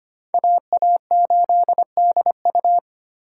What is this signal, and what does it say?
Morse code, heard as a keyed tone, about 30 wpm, 700 Hz. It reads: AA8DU
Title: Speed Racing - USA 2x2 Call Signs 25wpm